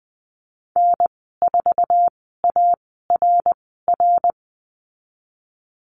Morse code, heard as a keyed tone, about 20 wpm, spaced as for 14 wpm, 700 Hz. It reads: N4ARR
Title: 062 Lesson 11g - Letter h - Call Signs 20wpm